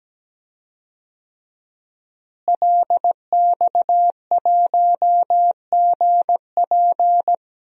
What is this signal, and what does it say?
Morse code, heard as a keyed tone, about 17 wpm, 700 Hz. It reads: LX1GP
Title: DXCC-100 Call Signs - 5 Characters - Repeated 2x - 17wpm